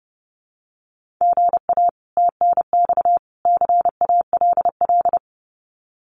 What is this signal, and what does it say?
Morse code, heard as a keyed tone, about 30 wpm, 700 Hz. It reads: GA TNX CALL
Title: Common QSO Phrases 30wpm